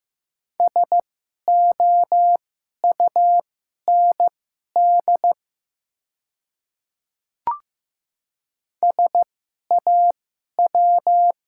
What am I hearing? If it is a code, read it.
Morse code, heard as a keyed tone, about 15 wpm, spaced as for 11 wpm, 700 Hz. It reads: SOUND SAW
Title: 2x Character Spacing - Top 300 Words - 15wpm